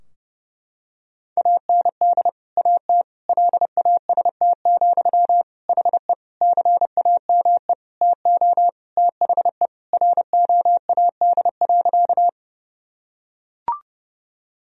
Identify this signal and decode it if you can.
Morse code, heard as a keyed tone, about 32 wpm, 700 Hz. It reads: AND AT LAST, HE CAME TO THE ROAD.
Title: Sentences from Top 1000 Words - Review Part 2 - 30wpm